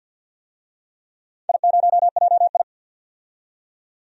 Morse code, heard as a keyed tone, about 53 wpm, 700 Hz. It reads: I0JI
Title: DXCC-100 Call Signs - 4 Characters - 50wpm